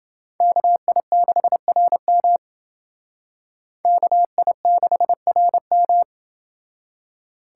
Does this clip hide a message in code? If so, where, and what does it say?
Yes: Morse code from 0.4 to 6.0 s, heard as a keyed tone, about 29 wpm, 700 Hz. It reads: KI6RM KI6RM